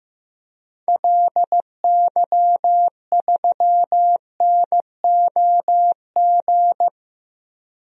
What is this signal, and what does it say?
Morse code, heard as a keyed tone, about 15 wpm, 700 Hz. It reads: LY3NOG
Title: DXCC-100 Call Signs - 6 Characters - 15wpm